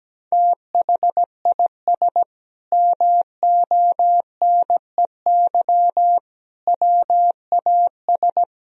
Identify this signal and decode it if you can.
Morse code, heard as a keyed tone, about 17 wpm, 700 Hz. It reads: THIS MONEY WAS
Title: Sentences from Top 400 Words and Abbreviations 17wpm